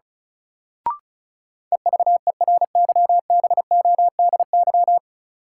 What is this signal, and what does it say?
Morse code, heard as a keyed tone, about 36 wpm, 700 Hz. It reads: EVERYBODY
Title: Sentences from Top 1400 Words - New Word Drill Alt - 35wpm